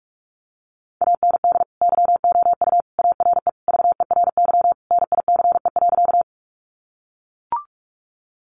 Morse code, heard as a keyed tone, about 44 wpm, 700 Hz. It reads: AND YOU ARE VERY NICE.
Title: Sentences from Top 1100 Words 45wpm